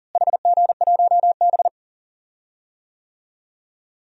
Morse code, heard as a keyed tone, about 39 wpm, 700 Hz. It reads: HG1B
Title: DXCC-100 Call Signs - 4 Characters - Repeated 3x - 40wpm